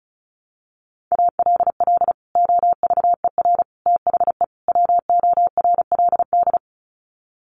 Morse code, heard as a keyed tone, about 34 wpm, 700 Hz. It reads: ALL OVER THE WORLD
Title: Sets of 4 Words 35wpm